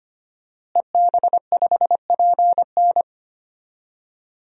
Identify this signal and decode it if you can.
Morse code, heard as a keyed tone, about 25 wpm, 700 Hz. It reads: EB5PN
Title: DXCC-100 Call Signs - 5 Characters - Repeated 2x - 25wpm